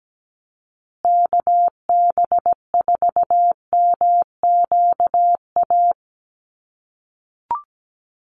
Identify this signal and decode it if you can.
Morse code, heard as a keyed tone, about 17 wpm, 700 Hz. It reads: KB4MQA
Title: USA Call Signs Repeated 3x 17wpm